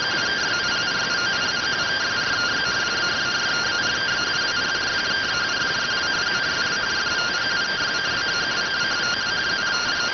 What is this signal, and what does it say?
Two-tone teleprinter signal, about 170 Hz shift.